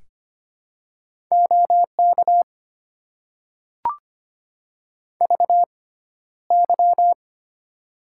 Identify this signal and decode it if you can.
Morse code, heard as a keyed tone, about 25 wpm, spaced as for 13 wpm, 700 Hz. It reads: OK VY
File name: Top 90 QSO Elements - 6x Character Spacing - 25wpm